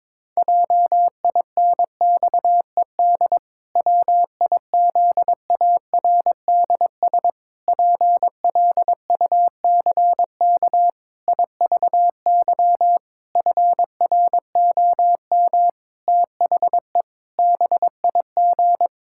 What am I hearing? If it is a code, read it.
Morse code, heard as a keyed tone, about 22 wpm, 700 Hz. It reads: JINXED WIZARDS PLUCK IVY FROM THE BIG
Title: Pangram Challenge v2 - 22wpm